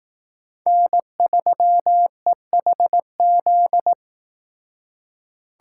Morse code, heard as a keyed tone, about 18 wpm, 700 Hz. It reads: N3EHZ
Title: Speed Racing - USA 1x3 Call Signs 15wpm